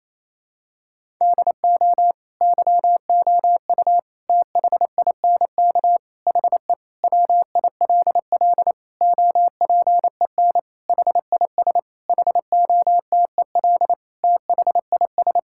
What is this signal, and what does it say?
Morse code, heard as a keyed tone, about 28 wpm, 700 Hz. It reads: DO YOU THINK HE WILL OPEN HIS HOTEL THIS
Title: Sentences from Top 1600 Words 28wpm